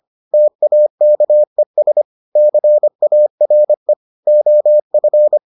Morse code, heard as a keyed tone, about 25 wpm, 600 Hz. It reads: TAKES CARE OF